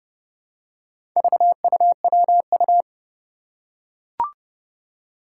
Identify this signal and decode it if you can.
Morse code, heard as a keyed tone, about 30 wpm, 690 Hz. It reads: VUWU
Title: Four Challenging Letter-Number Combinations 30wpm